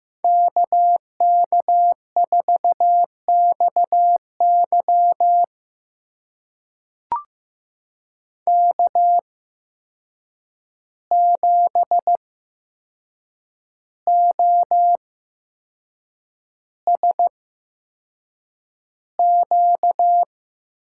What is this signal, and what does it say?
Morse code, heard as a keyed tone, about 15 wpm, spaced as for 6 wpm, 700 Hz. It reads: KK4XY K7OSQ
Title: USA Call Signs - 5 Characters - 8x Character Spacing - 15wpm